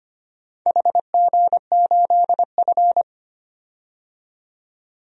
Morse code, heard as a keyed tone, about 25 wpm, 700 Hz. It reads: HG8F
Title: DXCC-100 Call Signs - 4 Characters - 25wpm